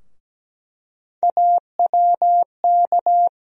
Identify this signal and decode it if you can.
Morse code, heard as a keyed tone, about 17 wpm, 700 Hz. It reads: AWK
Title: USA Call Sign Postfixes - 3 Letters 17wpm